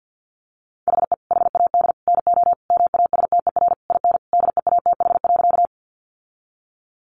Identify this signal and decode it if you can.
Morse code, heard as a keyed tone, about 50 wpm, 700 Hz. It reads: HE HAD NO MASTER IN DEATH.
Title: Sentences from Top 1300 Words 50wpm